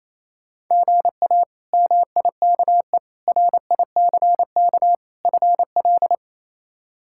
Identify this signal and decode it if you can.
Morse code, heard as a keyed tone, about 28 wpm, 700 Hz. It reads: GA MIKE RICK FL
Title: K1USN SST QSO Practice - 28wpm